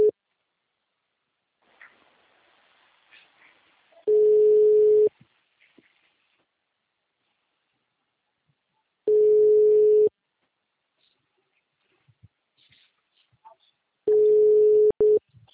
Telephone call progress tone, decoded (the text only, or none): ringback tone